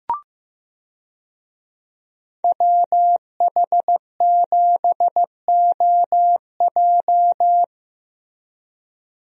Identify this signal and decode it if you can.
Morse code, heard as a keyed tone, about 15 wpm, 700 Hz. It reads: WH7OJ
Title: USA Call Signs - 5 Characters - Repeated 3x - 15wpm